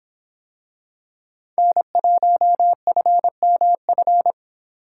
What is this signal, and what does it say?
Morse code, heard as a keyed tone, about 24 wpm, 700 Hz. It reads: N1FMF